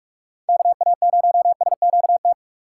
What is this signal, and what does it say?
Morse code, heard as a keyed tone, about 47 wpm, 700 Hz. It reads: KA0SQT
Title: USA Call Signs - 6 Characters - 3x Character Spacing - 45wpm